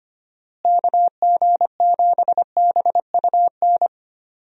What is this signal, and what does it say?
Morse code, heard as a keyed tone, about 25 wpm, 700 Hz. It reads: KG7BUN
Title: USA Call Signs - 6 Characters - Repeated 2x - 25wpm